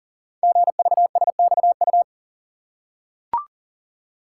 Morse code, heard as a keyed tone, about 38 wpm, 700 Hz. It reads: GVSXU
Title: Code Groups - 5 Letters - 40wpm